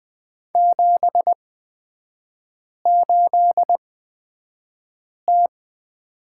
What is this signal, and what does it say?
Morse code, heard as a keyed tone, about 20 wpm, spaced as for 5 wpm, 700 Hz. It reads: 78T
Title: 243 Lesson 39e - Letter z - Three Characters 20wpm